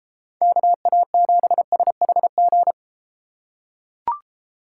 Morse code, heard as a keyed tone, about 32 wpm, 700 Hz. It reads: KA7SHG